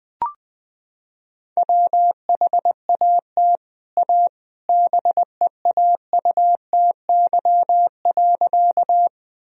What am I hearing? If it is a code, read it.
Morse code, heard as a keyed tone, about 20 wpm, 700 Hz. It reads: WHAT A BEAUTY.
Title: Sentences from Top 1000 Words - Review Part 2 - 20wpm